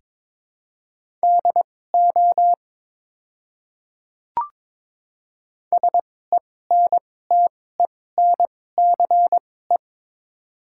Morse code, heard as a keyed tone, about 22 wpm, spaced as for 16 wpm, 690 Hz. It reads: DO SENTENCE
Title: 2x Character Spacing - Top 200 Words - 22wpm